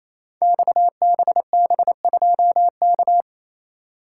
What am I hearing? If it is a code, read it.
Morse code, heard as a keyed tone, about 27 wpm, 700 Hz. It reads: XBB2K
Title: Code Groups - 5 Letter:Numbers - 28wpm